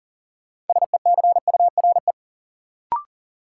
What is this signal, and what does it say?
Morse code, heard as a keyed tone, about 40 wpm, 700 Hz. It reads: SECURE
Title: Sentences from Top 1200 Words - New Word Drill - 40wpm